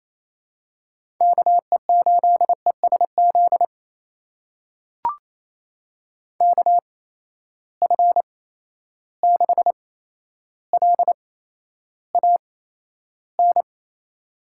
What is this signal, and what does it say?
Morse code, heard as a keyed tone, about 28 wpm, spaced as for 12 wpm, 700 Hz. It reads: KE8ESZ KF6LAN
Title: USA Call Signs - 6 Characters - 8x Character Spacing - 28wpm